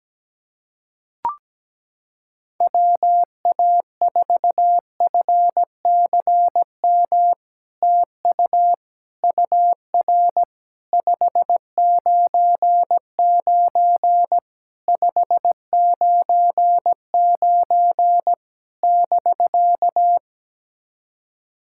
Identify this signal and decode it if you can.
Morse code, heard as a keyed tone, about 17 wpm, 700 Hz. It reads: WA4FCM TU UR 599 599 <BK>
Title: SOTA QSO Practice 17wpm